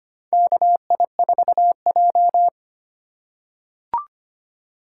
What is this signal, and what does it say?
Morse code, heard as a keyed tone, about 25 wpm, 700 Hz. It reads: KI4J
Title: USA Call Signs - 2x Character Spacing - 25wpm